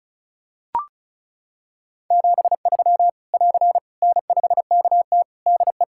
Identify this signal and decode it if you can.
Morse code, heard as a keyed tone, about 34 wpm, 700 Hz. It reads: 73 <AR> N5KT DE
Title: Practice QSO - Exchange 4 - Ending - 35wpm